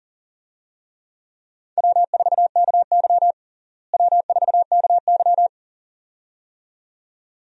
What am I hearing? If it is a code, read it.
Morse code, heard as a keyed tone, about 40 wpm, 700 Hz. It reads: W4KY W4KY
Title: USA Call Signs - 4 Characters - Repeated 2x - 40wpm